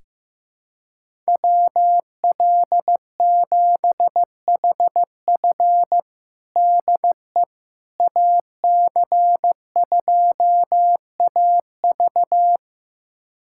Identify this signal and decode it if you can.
Morse code, heard as a keyed tone, about 15 wpm, 700 Hz. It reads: WL7HF DE AC2AV
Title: USA Callsign DE Callsign - 5 Characters - 15wpm